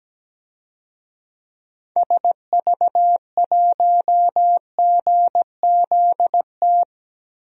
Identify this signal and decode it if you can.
Morse code, heard as a keyed tone, about 17 wpm, 700 Hz. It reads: SV1GZT
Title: DXCC-100 Call Signs - 6 Characters - Repeated 3x - 17wpm